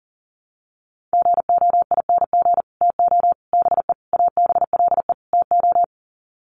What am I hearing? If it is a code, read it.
Morse code, heard as a keyed tone, about 40 wpm, 700 Hz. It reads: GOING TO BE ABLE TO
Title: Sets of 5 Words 40wpm